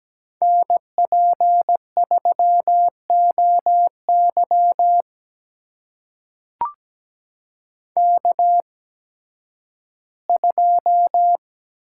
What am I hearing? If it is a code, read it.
Morse code, heard as a keyed tone, about 17 wpm, spaced as for 11 wpm, 700 Hz. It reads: NP3OY K2
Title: USA Call Signs - 5 Characters - 8x Character Spacing - 17wpm